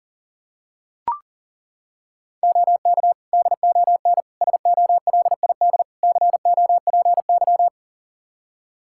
Morse code, heard as a keyed tone, about 40 wpm, 700 Hz. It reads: OK DON SOLID COPY
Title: Practice QSO - Exchange 3 - More Info - 40wpm